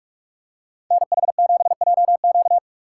Morse code, heard as a keyed tone, about 44 wpm, 700 Hz. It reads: NH7JQ
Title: USA Call Signs - 5 Characters - 3x Character Spacing - 45wpm